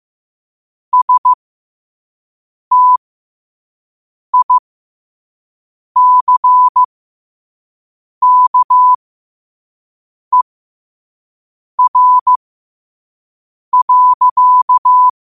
Morse code, heard as a keyed tone, about 15 wpm, spaced as for 5 wpm, 1000 Hz. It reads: STICKER.
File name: qotdfiles / 5wpm